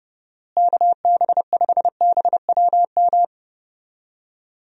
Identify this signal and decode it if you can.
Morse code, heard as a keyed tone, about 32 wpm, 700 Hz. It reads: KB5BWM
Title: USA Call Signs - 6 Characters - Repeated 2x - 30wpm